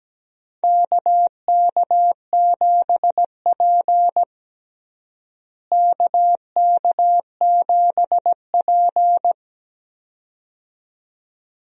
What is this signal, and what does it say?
Morse code, heard as a keyed tone, about 17 wpm, 700 Hz. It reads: KK7P KK7P
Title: USA Call Signs - 4 Characters - Repeated 2x - 17wpm